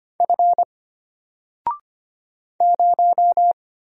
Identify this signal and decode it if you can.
Morse code, heard as a keyed tone, about 25 wpm, 700 Hz. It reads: F 0
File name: Single-Letter-Number-25wpm